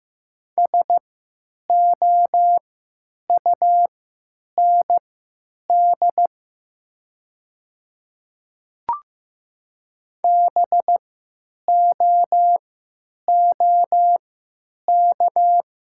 Morse code, heard as a keyed tone, about 15 wpm, spaced as for 9 wpm, 700 Hz. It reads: SOUND BOOK
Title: 3x Character Spacing - Top 500 Words - 15wpm